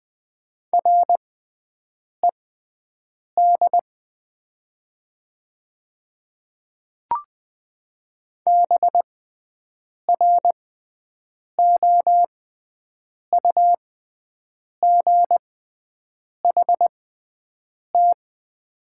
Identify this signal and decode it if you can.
Morse code, heard as a keyed tone, about 20 wpm, spaced as for 7 wpm, 700 Hz. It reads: RED BROUGHT
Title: 6x Character Spacing - Top 500 Words - 20wpm